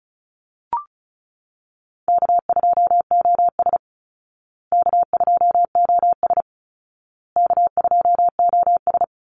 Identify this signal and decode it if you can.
Morse code, heard as a keyed tone, about 34 wpm, 700 Hz. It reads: K2OS K2OS K2OS
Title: Easy Call Signs Repeated 3x - 35wpm